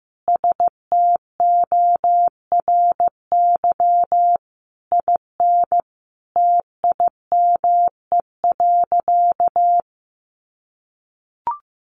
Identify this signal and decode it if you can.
Morse code, heard as a keyed tone, about 15 wpm, 700 Hz. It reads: STORY IN TIME.
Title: Sentences from Top 1400 Words 15wpm